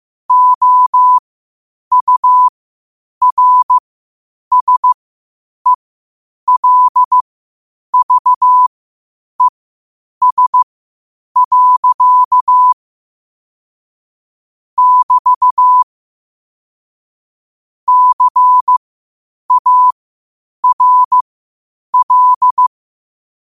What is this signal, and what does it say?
Morse code, heard as a keyed tone, about 15 wpm, spaced as for 8 wpm, 1000 Hz. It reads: OURSELVES. <BT> CARL
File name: qotdfiles / 8wpm